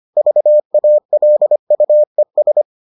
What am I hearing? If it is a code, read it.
Morse code, heard as a keyed tone, about 25 wpm, 600 Hz. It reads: VALUES